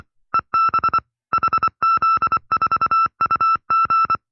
Morse code, heard as a keyed tone, about 24 wpm, 1330 Hz. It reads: EB HZ4UG
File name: FreeFiles / CW-practice / 20WPM-side2